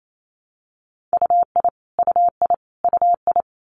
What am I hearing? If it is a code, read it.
Morse code, heard as a keyed tone, about 28 wpm, 700 Hz. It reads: UI UI UI
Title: Multi-letter Phonemes Repeated 3x 28wpm